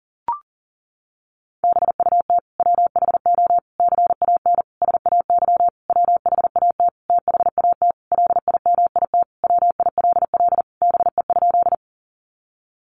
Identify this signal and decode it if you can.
Morse code, heard as a keyed tone, about 40 wpm, 700 Hz. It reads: BUT WHO CAN SAY WHAT THAT LIMIT WILL BE?
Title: Sentences from Top 1500 Words 40wpm